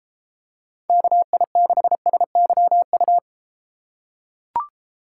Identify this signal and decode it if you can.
Morse code, heard as a keyed tone, about 32 wpm, 700 Hz. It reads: KI6SYU